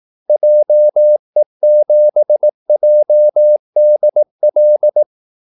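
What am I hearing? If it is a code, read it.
Morse code, heard as a keyed tone, about 18 wpm, 600 Hz. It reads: JE7JDL
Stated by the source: Morse call-sign practice at 18 wpm.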